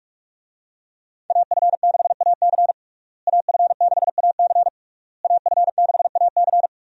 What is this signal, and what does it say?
Morse code, heard as a keyed tone, about 46 wpm, 700 Hz. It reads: AF6AC AF6AC AF6AC
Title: USA Call Signs - 5 Characters - Repeated 3x - 45wpm